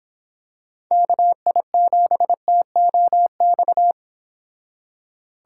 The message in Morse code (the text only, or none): KI7TOX